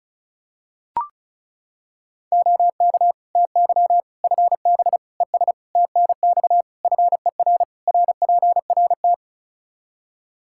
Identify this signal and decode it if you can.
Morse code, heard as a keyed tone, about 36 wpm, 700 Hz. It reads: OK TY FB ES TNX FER RPRT
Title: K1USN SST QSO Practice 35wpm